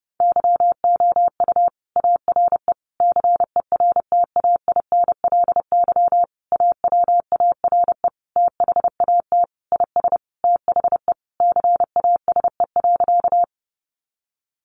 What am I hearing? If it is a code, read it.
Morse code, heard as a keyed tone, about 30 wpm, 700 Hz. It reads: YOU ARE CERTAINLY AWARE THAT IS THE CASE.
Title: Sentences from Top 1200 Words 30wpm